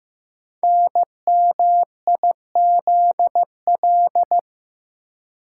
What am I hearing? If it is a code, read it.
Morse code, heard as a keyed tone, about 15 wpm, 700 Hz. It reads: NMIZL
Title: Code Groups - 5 Letters - 15wpm